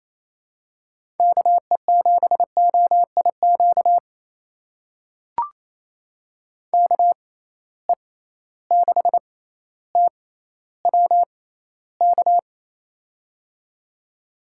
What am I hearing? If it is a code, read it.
Morse code, heard as a keyed tone, about 28 wpm, spaced as for 14 wpm, 700 Hz. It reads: KE7OIQ KE6TWK
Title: USA Call Signs - 6 Characters - 6x Character Spacing - 28wpm